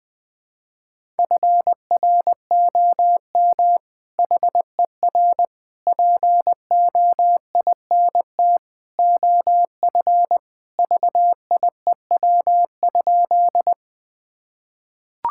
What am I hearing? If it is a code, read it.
Morse code, heard as a keyed tone, about 20 wpm, 700 Hz. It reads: FROM HER POINT OF VIEW?
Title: Sentences from Top 1000 Words - Review Part 6 - 20wpm